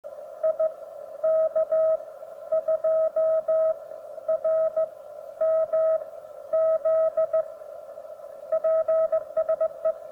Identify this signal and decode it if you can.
Morse code, heard as a keyed tone, about 16 wpm, spaced as for 11 wpm, 640 Hz. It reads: IK2RMZ PSE